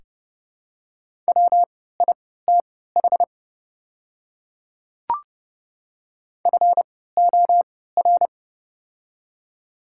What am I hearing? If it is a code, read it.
Morse code, heard as a keyed tone, about 30 wpm, spaced as for 17 wpm, 700 Hz. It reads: WITH FOR
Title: 3x Character Spacing - Top 100 Words - 30wpm